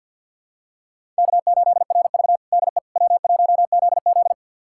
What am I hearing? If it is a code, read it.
Morse code, heard as a keyed tone, about 48 wpm, 700 Hz. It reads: K8RV DE W1ZZ
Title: USA Callsign DE Callsign - 4 Characters - 50wpm